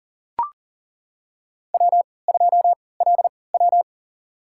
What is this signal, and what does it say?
Morse code, heard as a keyed tone, about 40 wpm, spaced as for 23 wpm, 700 Hz. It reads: W2LW
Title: USA Call Signs - 4 Characters - 3x Character Spacing - 40wpm